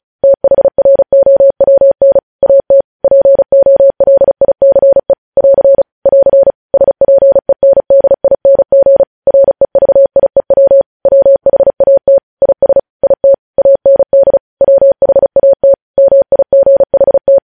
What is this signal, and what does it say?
Morse code, heard as a keyed tone, about 36 wpm, 550 Hz. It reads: THROWN AT POLICE <AR> <AR> SPENDING REVIEW WHAT IS IT AND WHAT MIGHT
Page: news headlines in morse code at 35 wpm